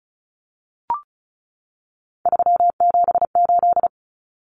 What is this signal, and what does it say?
Morse code, heard as a keyed tone, about 34 wpm, 700 Hz. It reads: 378
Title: Common Cut Numbers - 3-Digit 35wpm